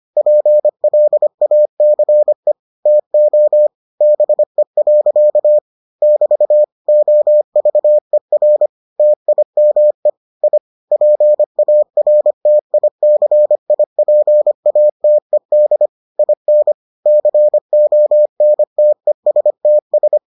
Morse code, heard as a keyed tone, about 25 wpm, 600 Hz. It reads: PLACE TO BE. <BT> OVER TIME I PARTICIPATED IN CONTESTS